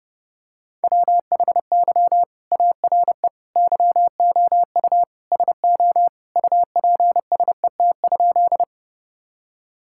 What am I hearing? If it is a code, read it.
Morse code, heard as a keyed tone, about 30 wpm, 700 Hz. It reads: WHY ARE YOU SO UPSET?
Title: Sentences from Top 1000 Words - Review Part 5 - 30wpm